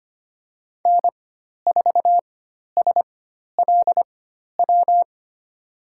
Morse code, heard as a keyed tone, about 25 wpm, spaced as for 12 wpm, 700 Hz. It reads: N4SLW
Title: USA Call Signs - 5 Characters - 4x Character Spacing - 25wpm